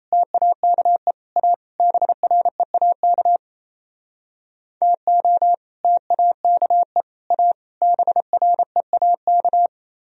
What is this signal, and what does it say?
Morse code, heard as a keyed tone, about 36 wpm, 690 Hz. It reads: TAKE A BREAK TO TAKE A BREAK